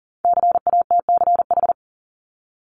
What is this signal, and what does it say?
Morse code, heard as a keyed tone, about 40 wpm, 700 Hz. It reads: CATCH
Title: Top 1251-1500 Words 40wpm